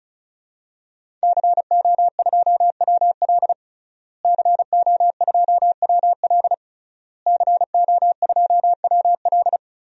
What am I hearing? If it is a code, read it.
Morse code, heard as a keyed tone, about 36 wpm, 700 Hz. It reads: CO2WL CO2WL CO2WL
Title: DXCC-100 Call Signs - 5 Characters - Repeated 3x - 35wpm